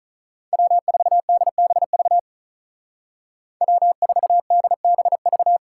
Morse code, heard as a keyed tone, about 43 wpm, 700 Hz. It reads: W4DBV W4DBV
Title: Speed Racing - USA 1x3 Call Signs 35wpm